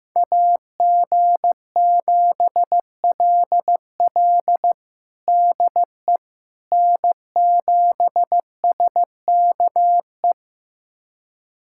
Morse code, heard as a keyed tone, about 15 wpm, 700 Hz. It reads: AG7LL DE N7SKE
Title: USA Callsign DE Callsign - 5 Characters - 15wpm